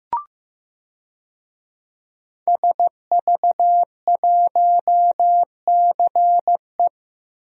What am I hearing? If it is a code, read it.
Morse code, heard as a keyed tone, about 15 wpm, 700 Hz. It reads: SV1CE
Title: DXCC-100 Call Signs - 5 Characters - Repeated 2x - 15wpm